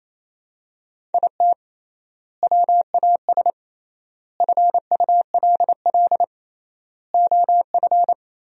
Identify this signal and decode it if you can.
Morse code, heard as a keyed tone, about 28 wpm, 700 Hz. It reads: IT WAS FULL OF
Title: 3x Word-Spacing - Sentences from Top 500 Words - 28wpm